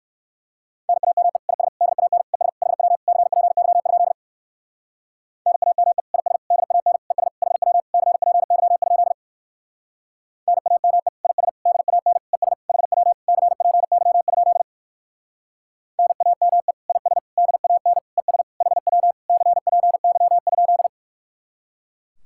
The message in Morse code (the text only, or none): NAME IS DAN ES HW CPY? NAME IS DAN ES HW CPY? NAME IS DAN ES HW CPY? NAME IS DAN ES HW CPY?